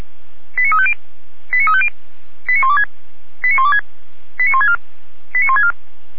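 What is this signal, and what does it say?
Five-tone selective call (ZVEI), heard as quick runs of five tones, tones of about 70 ms. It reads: 89350 89350 89126 89126 89164 89164
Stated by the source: Five-tone ZVEI selective call received on an SDR.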